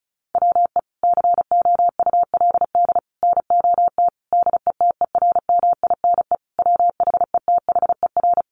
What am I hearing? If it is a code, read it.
Morse code, heard as a keyed tone, about 34 wpm, 700 Hz. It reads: WE COULD NOT DETERMINE WHETHER
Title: Sentences from Top 1300 Words 35wpm